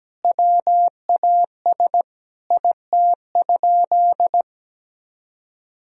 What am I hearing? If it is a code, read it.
Morse code, heard as a keyed tone, about 17 wpm, 700 Hz. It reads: WAS IT?
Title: Sentences from Top 1000 Words - Review Part 4 - 17wpm